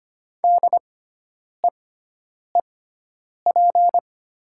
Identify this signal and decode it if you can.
Morse code, heard as a keyed tone, about 25 wpm, spaced as for 9 wpm, 700 Hz. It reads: DEEP
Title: 6x Character Spacing - Top 500 Words - 25wpm